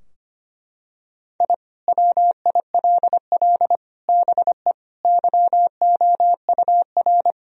Morse code, heard as a keyed tone, about 25 wpm, 700 Hz. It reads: I WILL BE YOUR
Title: Sentences from Top 1000 Words - Review Part 4 - 25wpm